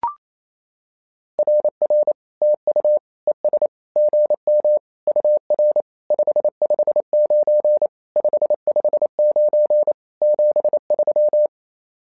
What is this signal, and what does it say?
Morse code, heard as a keyed tone, about 28 wpm, 600 Hz. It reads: RR TU ES GM UR 559 559 73
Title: SOTA QSO Practice 28wpm